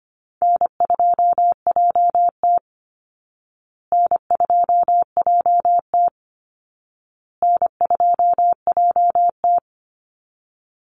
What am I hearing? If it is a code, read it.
Morse code, heard as a keyed tone, about 25 wpm, 700 Hz. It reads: N2JT N2JT N2JT
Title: Easy Call Signs Repeated 3x - 25wpm